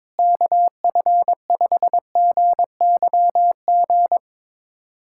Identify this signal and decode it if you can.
Morse code, heard as a keyed tone, about 21 wpm, 700 Hz. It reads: KF5GYG
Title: USA Callsign DE Callsign 22wpm